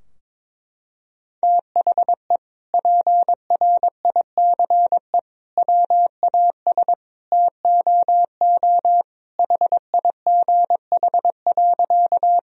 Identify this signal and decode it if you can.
Morse code, heard as a keyed tone, about 22 wpm, 700 Hz. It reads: THE PRICE WAS TOO HIGH.
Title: Sentences from Top 1000 Words - Review Part 4 - 22wpm